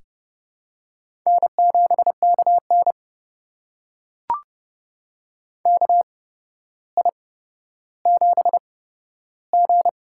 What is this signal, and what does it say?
Morse code, heard as a keyed tone, about 29 wpm, spaced as for 13 wpm, 700 Hz. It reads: N7KN KI7G
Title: USA Call Signs - 8x Character Spacing - 30wpm